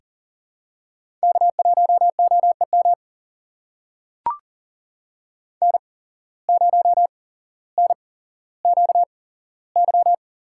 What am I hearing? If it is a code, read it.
Morse code, heard as a keyed tone, about 41 wpm, spaced as for 17 wpm, 700 Hz. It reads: K1OEM N0NQY
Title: USA Call Signs - 5 Characters - 8x Character Spacing - 40wpm